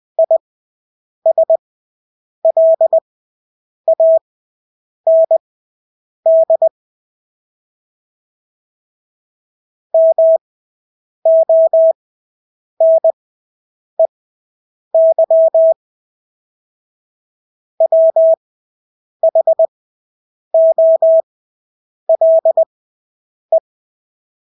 Morse code, heard as a keyed tone, about 20 wpm, spaced as for 8 wpm, 650 Hz. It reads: ISLAND MONEY WHOLE
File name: Engelske ord 20/8 wpm